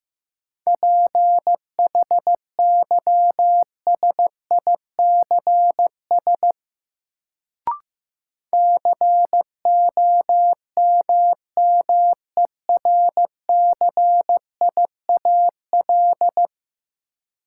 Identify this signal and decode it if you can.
Morse code, heard as a keyed tone, about 15 wpm, 700 Hz. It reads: PHYSICS COMMERCIAL
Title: Sentences from Top 1000 Words - Review Part 5 - New Word Drill - 15wpm